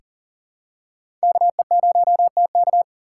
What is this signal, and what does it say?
Morse code, heard as a keyed tone, about 40 wpm, 700 Hz. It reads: KE0TK
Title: USA Call Signs - 5 Characters - 6x Character Spacing - 40wpm